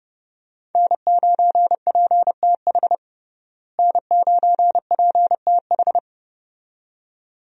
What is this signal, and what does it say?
Morse code, heard as a keyed tone, about 30 wpm, 700 Hz. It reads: N9PTH N9PTH
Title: USA Call Signs - 5 Characters - Repeated 2x - 30wpm